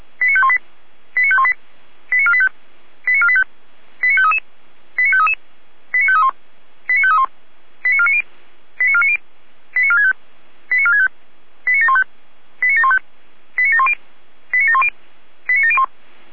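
Five-tone selective call (ZVEI), heard as quick runs of five tones, tones of about 70 ms. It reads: 89518 89518 89485 89485 89533 89533 89531 89531 89490 89490 89465 89465 89715 89715 89710 89710 89701
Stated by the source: ZVEI five-tone call 89416 received on an SDR, from the feuerwehr-orlamuende.org fire brigade audio archive.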